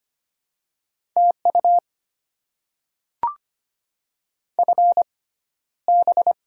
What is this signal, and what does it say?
Morse code, heard as a keyed tone, about 25 wpm, spaced as for 13 wpm, 700 Hz. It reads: TU FB
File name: Top 20 QSO Elements - 6x Character Spacing - 25wpm